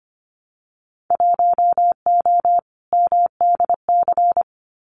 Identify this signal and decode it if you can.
Morse code, heard as a keyed tone, about 25 wpm, 700 Hz. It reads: 1O MDC
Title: Winter Field Day Exchanges 25wpm